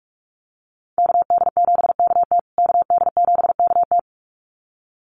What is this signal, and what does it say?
Morse code, heard as a keyed tone, about 44 wpm, 700 Hz. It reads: KD7KT KD7KT
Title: USA Call Signs Repeated 2x 45wpm